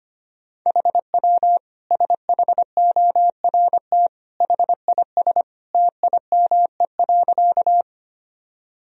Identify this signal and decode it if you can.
Morse code, heard as a keyed tone, about 25 wpm, 700 Hz. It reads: HW SHORT HIS TIME.
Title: Sentences from Top 400 Words and Abbreviations 25wpm